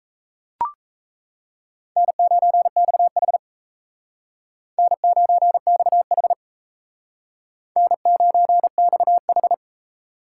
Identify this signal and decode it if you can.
Morse code, heard as a keyed tone, about 35 wpm, 700 Hz. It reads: N9XH N9XH N9XH
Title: Speed Racing - USA 1x2 Call Signs 28wpm